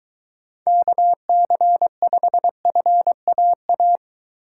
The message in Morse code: KC5FAA

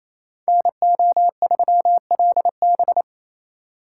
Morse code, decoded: NO3LB

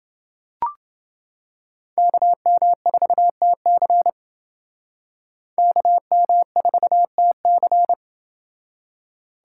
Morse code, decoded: KM4TC KM4TC

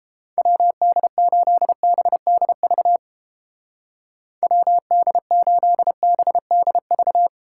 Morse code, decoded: WD8BDV WD8BDV

